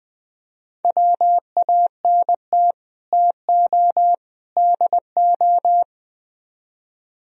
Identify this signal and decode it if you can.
Morse code, heard as a keyed tone, about 20 wpm, 700 Hz. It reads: WANT TO DO
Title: Speed-Racing - Sets of 3 Words 17wpm